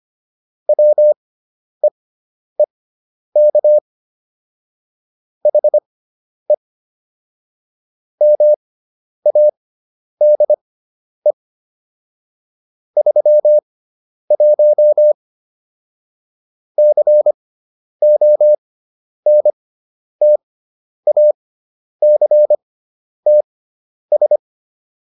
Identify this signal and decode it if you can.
Morse code, heard as a keyed tone, about 25 wpm, spaced as for 10 wpm, 600 Hz. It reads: WEEK HE MADE 31 CONTACTS